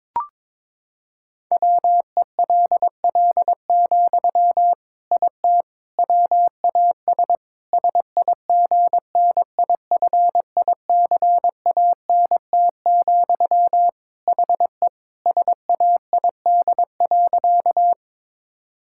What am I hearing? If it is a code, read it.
Morse code, heard as a keyed tone, about 22 wpm, 700 Hz. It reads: WELL, IT WAS SIGNIFICANT, HE SAID.
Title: Sentences from Top 1000 Words - Review Part 6 - 22wpm